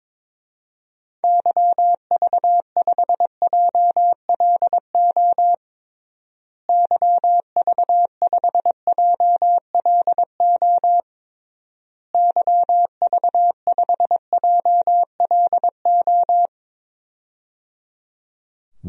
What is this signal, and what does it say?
Morse code, heard as a keyed tone, about 22 wpm, 700 Hz. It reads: YV5JLO YV5JLO YV5JLO
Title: DXCC-100 Call Signs - 6 Characters - Repeated 3x - 22wpm